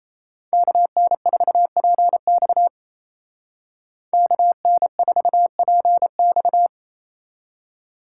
Morse code, heard as a keyed tone, about 36 wpm, 700 Hz. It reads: KN4PX KN4PX